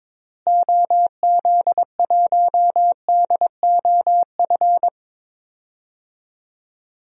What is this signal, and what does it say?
Morse code, heard as a keyed tone, about 22 wpm, 700 Hz. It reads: OZ1DOF
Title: DXCC-100 Call Signs - 6 Characters - Repeated 2x - 22wpm